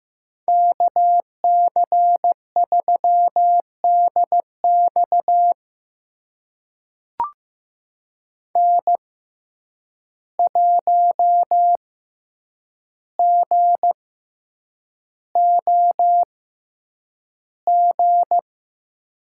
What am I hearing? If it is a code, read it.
Morse code, heard as a keyed tone, about 15 wpm, spaced as for 8 wpm, 700 Hz. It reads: KC3DX N1GOG
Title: USA Call Signs - 5 Characters - 6x Character Spacing - 15wpm